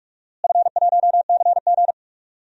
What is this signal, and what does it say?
Morse code, heard as a keyed tone, about 46 wpm, 700 Hz. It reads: F1CG
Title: DXCC-100 Call Signs - 4 Characters - Repeated 2x - 45wpm